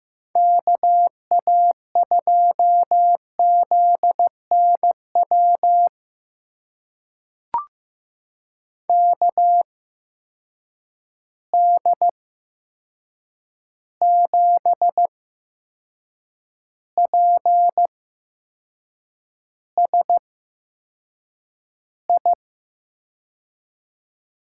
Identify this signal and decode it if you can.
Morse code, heard as a keyed tone, about 15 wpm, spaced as for 6 wpm, 700 Hz. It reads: KA2ZNW KD7PSI
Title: USA Call Signs - 6 Characters - 8x Character Spacing - 15wpm